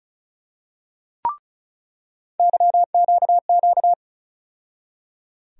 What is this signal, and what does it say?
Morse code, heard as a keyed tone, about 36 wpm, 700 Hz. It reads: YQQ
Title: USA Call Sign Postfixes - 3 Letters 35wpm